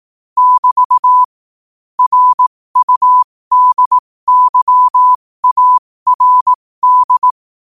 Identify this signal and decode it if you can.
Morse code, heard as a keyed tone, about 18 wpm, spaced as for 15 wpm, 1000 Hz. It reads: <BT> RUDYARD
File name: qotdfiles / 15wpm